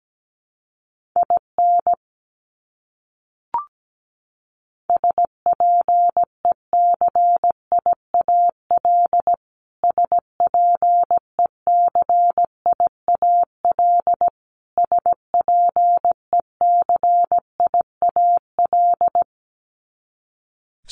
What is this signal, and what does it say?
Morse code, heard as a keyed tone, about 17 wpm, 700 Hz. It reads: IN SPECIAL SPECIAL SPECIAL
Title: Top 500 Words Repeated 17wpm